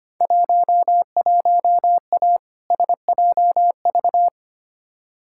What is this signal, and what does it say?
Morse code, heard as a keyed tone, about 25 wpm, 700 Hz. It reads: 11A SJV
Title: Field Day Exchanges Encore - 25wpm